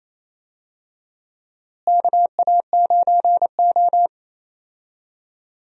Morse code, heard as a keyed tone, about 28 wpm, 700 Hz. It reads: KA9O
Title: USA Call Signs - 4 Characters - Repeated 2x - 28wpm